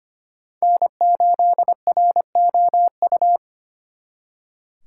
Morse code, heard as a keyed tone, about 31 wpm, 700 Hz. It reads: N8ROU